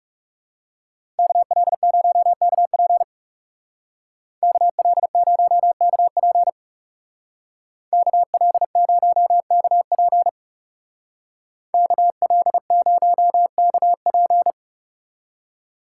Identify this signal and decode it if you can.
Morse code, heard as a keyed tone, about 39 wpm, 700 Hz. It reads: KL0KP KL0KP KL0KP KL0KP